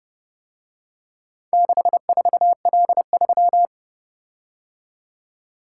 Morse code, heard as a keyed tone, about 29 wpm, 700 Hz. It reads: 64L3